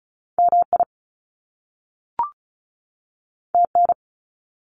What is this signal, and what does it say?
Morse code, heard as a keyed tone, about 36 wpm, 700 Hz. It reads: MI TN
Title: US States 35wpm